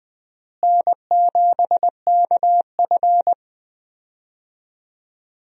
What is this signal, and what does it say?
Morse code, heard as a keyed tone, about 20 wpm, 700 Hz. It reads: N7KF
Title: USA Call Signs - 4 Characters - Repeated 3x - 20wpm